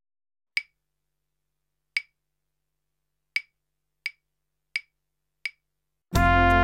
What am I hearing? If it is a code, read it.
There is no code.